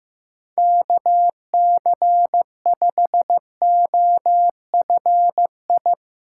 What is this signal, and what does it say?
Morse code, heard as a keyed tone, about 15 wpm, 700 Hz. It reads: KC5OFI
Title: USA Call Signs - 6 Characters - 6x Character Spacing - 15wpm